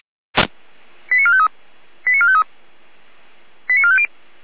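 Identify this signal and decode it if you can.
Five-tone selective call (ZVEI), heard as quick runs of five tones, tones of about 70 ms. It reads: 89453 89453 89450
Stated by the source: ZVEI five-tone selective call received by SDR.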